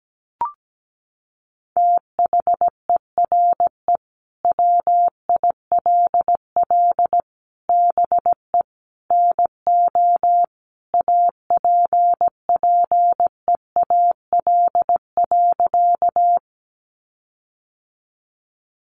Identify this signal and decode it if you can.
Morse code, heard as a keyed tone, about 17 wpm, 700 Hz. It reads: THERE WILL BE NO APPEAL.
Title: Sentences from Top 1300 Words 17wpm